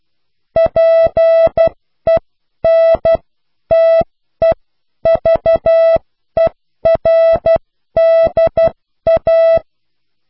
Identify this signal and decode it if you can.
Morse code, heard as a keyed tone, about 12 wpm, spaced as for 10 wpm, 660 Hz. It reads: PENTEVERDA
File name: cw / pseudo worden(10)